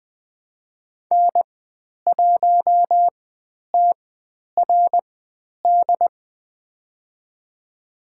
Morse code, heard as a keyed tone, about 20 wpm, spaced as for 10 wpm, 700 Hz. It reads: N1TRD
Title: 109 Lesson 18g - Letter m - Call Signs 20wpm